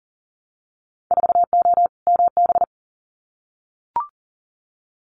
Morse code, heard as a keyed tone, about 40 wpm, 700 Hz. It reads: 4O MB
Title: Winter Field Day Exchanges Encore 40wpm